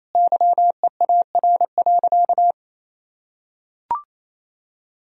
Morse code, heard as a keyed tone, about 28 wpm, 700 Hz. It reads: YEAR.
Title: Sentences from Top 1000 Words - Review Part 1 - 28wpm